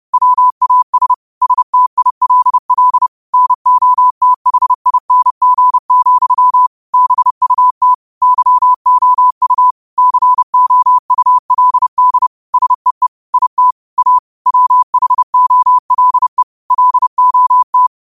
Morse code, heard as a keyed tone, about 30 wpm, 1000 Hz. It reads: WAS STILL NOTHING, BUT YOU COULD SEE IT A WHOLE LOT